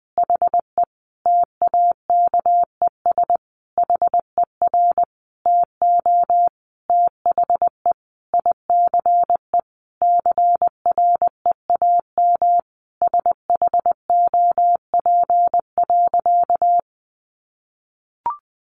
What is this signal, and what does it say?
Morse code, heard as a keyed tone, about 20 wpm, 700 Hz. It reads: HE TAKES HER TO THE ICE CREAM SHOP.
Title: Easy Phrases 20wpm